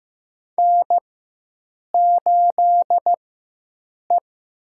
Morse code, heard as a keyed tone, about 15 wpm, 700 Hz. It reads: N 8 E
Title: USA Call Signs - 4 Characters - 4x Character Spacing - 15wpm